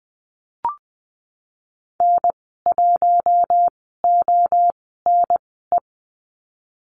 Morse code, heard as a keyed tone, about 20 wpm, spaced as for 15 wpm, 700 Hz. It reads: N1ONE
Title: 042 Lesson 8g - Number 1 - Call Signs 20wpm